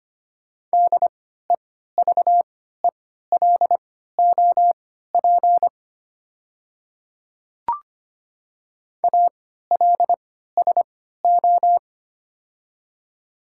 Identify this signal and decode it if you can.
Morse code, heard as a keyed tone, about 25 wpm, spaced as for 14 wpm, 690 Hz. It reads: DEVELOP ALSO
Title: 3x Character Spacing - Top 500 Words - 25wpm